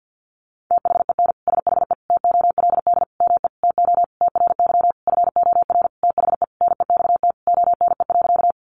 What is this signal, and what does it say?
Morse code, heard as a keyed tone, about 50 wpm, 700 Hz. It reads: THEN SHE TOLD ME TO TRY FOR THE NEXT ONE.
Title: Sentences from Top 400 Words 50wpm